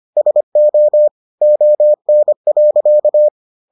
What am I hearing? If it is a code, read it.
Morse code, heard as a keyed tone, about 25 wpm, 600 Hz. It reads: SO ON.